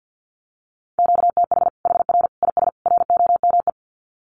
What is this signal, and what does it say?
Morse code, heard as a keyed tone, about 50 wpm, 700 Hz. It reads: QTH HR IS ROME
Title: Common QSO Phrases 50wpm